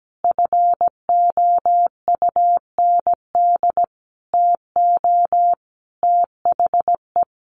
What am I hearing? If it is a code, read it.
Morse code, heard as a keyed tone, about 17 wpm, 700 Hz. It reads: FOUND TO THE
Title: Sentences from Top 1400 Words 17wpm